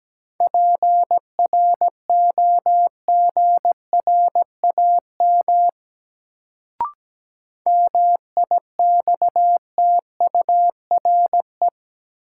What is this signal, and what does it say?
Morse code, heard as a keyed tone, about 17 wpm, 700 Hz. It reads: PROGRAM MIXTURE
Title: Sentences from Top 1500 Words - New Word Drill Alt - 17wpm